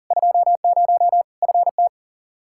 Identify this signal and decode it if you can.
Morse code, heard as a keyed tone, about 40 wpm, 700 Hz. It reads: 20 FT
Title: Practice QSO - Exchange 2 - Info - 40wpm